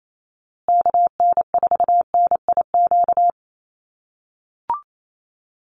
Morse code, heard as a keyed tone, about 28 wpm, 700 Hz. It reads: KN4NIQ
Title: USA Call Signs Repeated 3x 28wpm